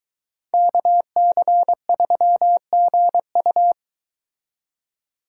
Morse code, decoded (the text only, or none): KC3GU